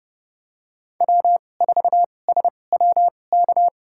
Morse code, heard as a keyed tone, about 30 wpm, spaced as for 22 wpm, 700 Hz. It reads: W4SWK
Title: USA Call Signs - 5 Characters - 2x Character Spacing - 30wpm